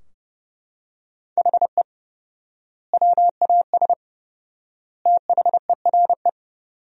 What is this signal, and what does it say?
Morse code, heard as a keyed tone, about 31 wpm, 700 Hz. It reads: HE WAS THERE
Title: 4x Word-Spacing - Sentences from Top 200 Words - 30wpm